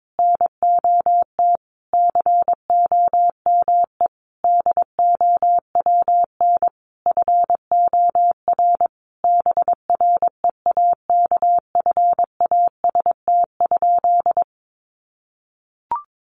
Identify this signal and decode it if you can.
Morse code, heard as a keyed tone, about 22 wpm, 700 Hz. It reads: NOT COME DOWN FOR BREAKFAST?
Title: Sentences from Top 1500 Words 22wpm